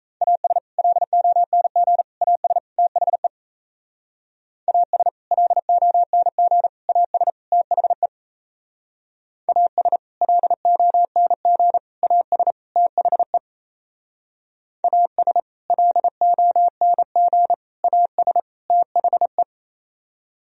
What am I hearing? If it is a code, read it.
Morse code, heard as a keyed tone, about 36 wpm, 700 Hz. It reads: AS LONG AS THE AS LONG AS THE AS LONG AS THE AS LONG AS THE